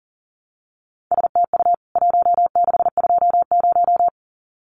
Morse code, heard as a keyed tone, about 40 wpm, 700 Hz. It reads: STU 1620
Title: CWT Exchange 40wpm